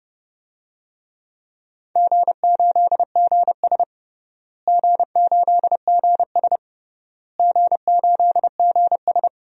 Morse code, heard as a keyed tone, about 32 wpm, 700 Hz. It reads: G8GS G8GS G8GS
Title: DXCC-100 Call Signs - 4 Characters - Repeated 3x - 30wpm